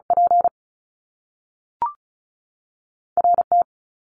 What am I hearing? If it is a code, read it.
Morse code, heard as a keyed tone, about 36 wpm, 700 Hz. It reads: P R T
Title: Top Two Letter Combinations in Words 35wpm